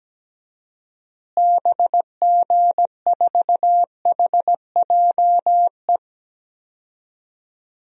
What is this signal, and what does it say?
Morse code, heard as a keyed tone, about 17 wpm, 700 Hz. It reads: BG4HJE
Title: DXCC-100 Call Signs - 6 Characters - Repeated 3x - 17wpm